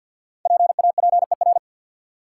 Morse code, heard as a keyed tone, about 50 wpm, 700 Hz. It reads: PAPER
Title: Five Letter Words - 50wpm